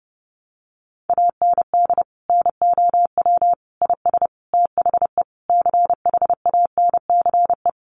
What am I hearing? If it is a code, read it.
Morse code, heard as a keyed tone, about 30 wpm, 700 Hz. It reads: AND NOW IS THE CHANCE
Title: Sentences from Top 800 Words 30wpm